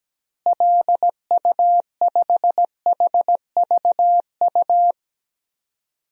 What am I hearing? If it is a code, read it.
Morse code, heard as a keyed tone, about 17 wpm, 700 Hz. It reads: LU5HVU
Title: DXCC-100 Call Signs - 6 Characters - Repeated 2x - 17wpm